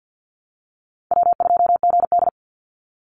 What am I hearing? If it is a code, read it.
Morse code, heard as a keyed tone, about 50 wpm, 700 Hz. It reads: W2GD
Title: Call Signs - 50wpm